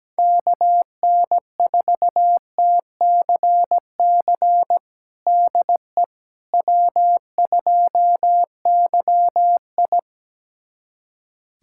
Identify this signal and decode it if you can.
Morse code, heard as a keyed tone, about 17 wpm, 700 Hz. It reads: KN4TCC DE W2YI
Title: USA Callsign DE Callsign 17wpm